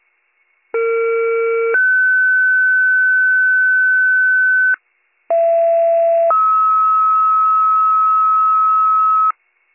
Two-tone page, heard as a one-second tone, then a three-second tone, about 460 then 1500 Hz, about 670 then 1290 Hz, with noise. It is heard from 0.7 to 9.3 s.